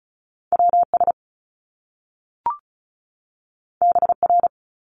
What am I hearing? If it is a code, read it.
Morse code, heard as a keyed tone, about 36 wpm, 700 Hz. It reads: WS BR
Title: Top Two Letter Combinations in Words 35wpm